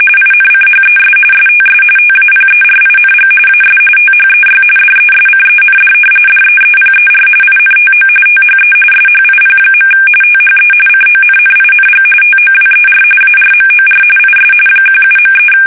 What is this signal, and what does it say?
RTTY, about 45.45 baud. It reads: INCHES STS TAPERING AS THEY DESCEND DEEPER INTO THE SHIP. THE CONNING TOWER ARMOR IS 17.3
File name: fsk850 / battleshipnj